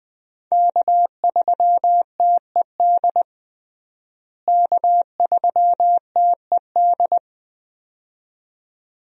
Morse code, heard as a keyed tone, about 20 wpm, 700 Hz. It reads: K3TED K3TED
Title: USA Call Signs - 5 Characters - Repeated 2x - 20wpm